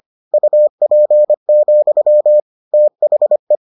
Morse code, heard as a keyed tone, about 25 wpm, 600 Hz. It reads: UP, THE